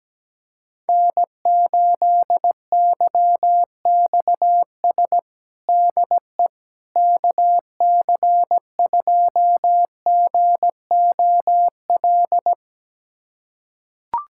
Morse code, heard as a keyed tone, about 17 wpm, 700 Hz. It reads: N8YXS DE KC2GOL
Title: USA Callsign DE Callsign 17wpm